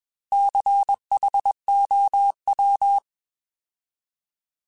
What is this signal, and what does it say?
Morse code, heard as a keyed tone, about 21 wpm, 800 Hz. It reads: CHOW